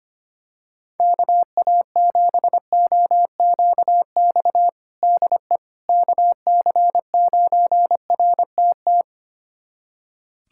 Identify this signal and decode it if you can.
Morse code, heard as a keyed tone, about 25 wpm, 700 Hz. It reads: KA7OQX DE KC9RTT
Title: USA Callsign DE Callsign - 6 Characters - 25wpm